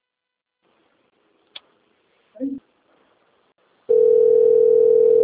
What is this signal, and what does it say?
Ringback tone (North America), 440 + 480 Hz.
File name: out- - - - - .37600